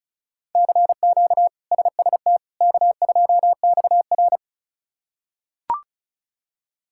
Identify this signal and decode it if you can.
Morse code, heard as a keyed tone, about 34 wpm, 700 Hz. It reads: CQ SST K2XR
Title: K1USN SST QSO Practice - 35wpm